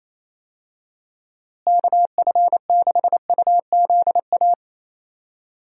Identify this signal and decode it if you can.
Morse code, heard as a keyed tone, about 28 wpm, 700 Hz. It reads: KF6UZA
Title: USA Call Signs - 6 Characters - Repeated 3x - 28wpm